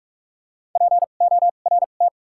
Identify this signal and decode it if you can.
Morse code, heard as a keyed tone, about 46 wpm, spaced as for 32 wpm, 700 Hz. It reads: PORT
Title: 2x Character Spacing - Top 500 Words - 45wpm